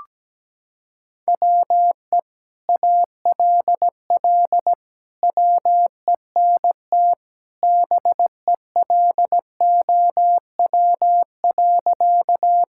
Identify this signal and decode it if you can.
Morse code, heard as a keyed tone, about 17 wpm, 700 Hz. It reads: WE ALL WENT BELOW.
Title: Sentences from Top 1000 Words - Review Part 5 - 17wpm